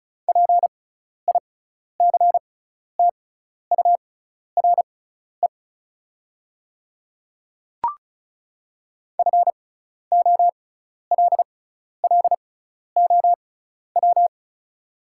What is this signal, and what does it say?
Morse code, heard as a keyed tone, about 34 wpm, spaced as for 12 wpm, 700 Hz. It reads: PICTURE FOLLOW
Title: 6x Character Spacing - Top 200 Words - 35wpm